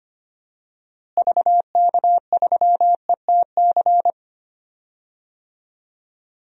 Morse code, heard as a keyed tone, about 25 wpm, 700 Hz. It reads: VK3ETC
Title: DXCC-100 Call Signs - 6 Characters - Repeated 3x - 25wpm